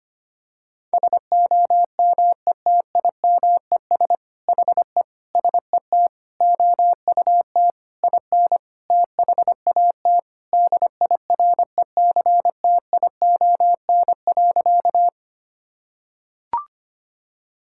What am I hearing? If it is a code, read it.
Morse code, heard as a keyed tone, about 25 wpm, 700 Hz. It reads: SOMETIMES HE SET OUT IN THAT DIRECTION.
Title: Sentences from Top 1000 Words - Review Part 5 - 25wpm